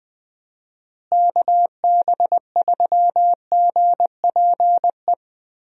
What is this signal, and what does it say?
Morse code, heard as a keyed tone, about 21 wpm, 700 Hz. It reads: KB3GPE
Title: USA Call Signs - 6 Characters - Repeated 2x - 20wpm